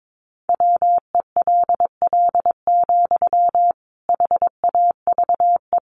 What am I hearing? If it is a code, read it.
Morse code, heard as a keyed tone, about 22 wpm, 700 Hz. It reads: WELL, HAVE
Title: Sentences from Top 800 Words 22wpm